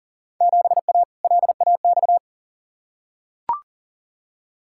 Morse code, heard as a keyed tone, about 40 wpm, 700 Hz. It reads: 7A LAX
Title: Field Day Exchanges Encore - 40wpm